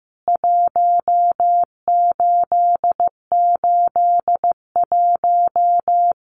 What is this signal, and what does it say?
Morse code, heard as a keyed tone, about 15 wpm, 700 Hz. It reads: 1881
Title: CWT Exchange 15wpm